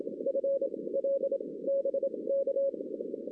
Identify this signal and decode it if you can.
Morse code, heard as a keyed tone, about 27 wpm, 550 Hz, with noise. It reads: F L B K
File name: 2016 arrl dx cw / 2016-ARRL-DX-CW-K5TR- - 0-9